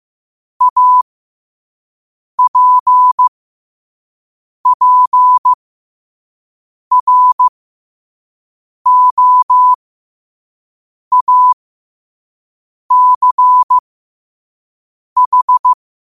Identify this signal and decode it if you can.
Morse code, heard as a keyed tone, about 15 wpm, spaced as for 5 wpm, 1000 Hz. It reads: APPROACH
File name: qotdfiles / 5wpm